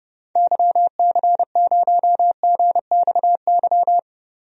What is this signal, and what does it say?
Morse code, heard as a keyed tone, about 30 wpm, 700 Hz. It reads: YC0GXY
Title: DXCC-100 Call Signs - 6 Characters - 30wpm